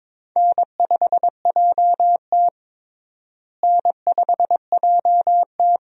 Morse code, heard as a keyed tone, about 21 wpm, 700 Hz. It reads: N5JT N5JT
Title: USA Call Signs - 4 Characters - Repeated 3x - 22wpm